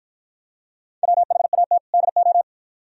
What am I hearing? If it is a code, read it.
Morse code, heard as a keyed tone, about 53 wpm, 700 Hz. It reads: WHAT DO